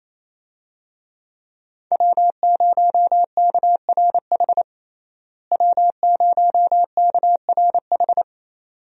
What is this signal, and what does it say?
Morse code, heard as a keyed tone, about 28 wpm, 700 Hz. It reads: W0KRH W0KRH
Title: USA Call Signs - 5 Characters - Repeated 2x - 28wpm